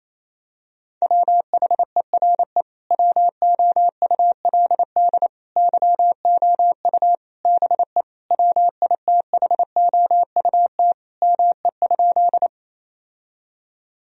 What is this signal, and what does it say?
Morse code, heard as a keyed tone, about 28 wpm, 700 Hz. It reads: WHERE WOULD YOU BE WITHOUT ME?
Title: Sentences from Top 1000 Words - Review Part 3 - 28wpm